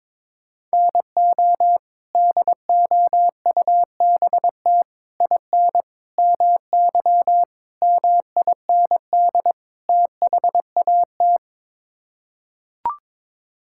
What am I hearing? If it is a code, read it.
Morse code, heard as a keyed tone, about 22 wpm, 700 Hz. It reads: NO DOUBT IN MY MIND THAT
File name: practice / 22wpm / Sets of 6 Words - 22wpm